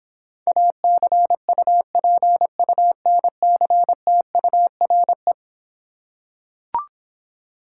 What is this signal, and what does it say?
Morse code, heard as a keyed tone, about 26 wpm, 700 Hz. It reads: ACUPUNCTURE